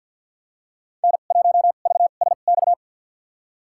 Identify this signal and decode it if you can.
Morse code, heard as a keyed tone, about 50 wpm, spaced as for 36 wpm, 700 Hz. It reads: N1VSX
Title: USA Call Signs - 5 Characters - 2x Character Spacing - 50wpm